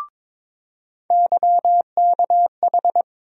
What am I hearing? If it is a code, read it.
Morse code, heard as a keyed tone, about 22 wpm, 700 Hz. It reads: YKH
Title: USA Call Sign Postfixes - 3 Letters 22wpm